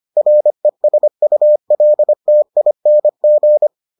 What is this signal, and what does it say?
Morse code, heard as a keyed tone, about 25 wpm, 600 Hz. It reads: RESULTING